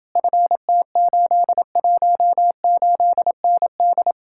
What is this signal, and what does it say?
Morse code, heard as a keyed tone, about 29 wpm, 700 Hz. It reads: FT818ND